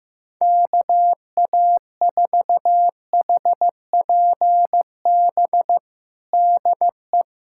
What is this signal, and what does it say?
Morse code, heard as a keyed tone, about 15 wpm, 700 Hz. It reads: KA4HPB DE
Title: USA Callsign DE Callsign 15wpm